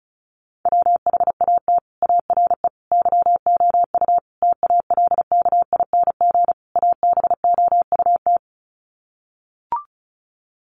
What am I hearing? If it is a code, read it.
Morse code, heard as a keyed tone, about 34 wpm, 700 Hz. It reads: WHAT ARE YOU TALKING ABOUT
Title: Sets of 5 Words 35wpm